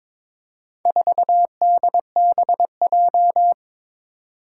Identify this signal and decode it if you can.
Morse code, heard as a keyed tone, about 22 wpm, 700 Hz. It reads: 4DBJ
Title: Four Challenging Letter-Number Combinations 22wpm